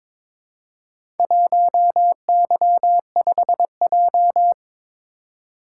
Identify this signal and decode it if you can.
Morse code, heard as a keyed tone, about 22 wpm, 690 Hz. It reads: 1Y5J
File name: practice / 22wpm / Four Letter-Number Combinations - 22wpm